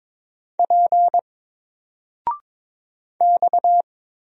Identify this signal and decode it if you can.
Morse code, heard as a keyed tone, about 22 wpm, 700 Hz. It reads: P X
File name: Single-Letter-Number-22wpm